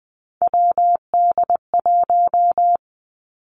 Morse code, heard as a keyed tone, about 20 wpm, 700 Hz. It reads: WD1
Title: Three Challenging Letter-Number Combinations 20wpm